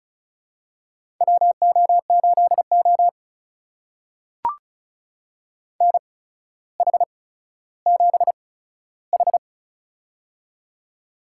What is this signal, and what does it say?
Morse code, heard as a keyed tone, about 36 wpm, spaced as for 15 wpm, 700 Hz. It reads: WO8O NH7H
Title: USA Call Signs - 4 Characters - 8x Character Spacing - 35wpm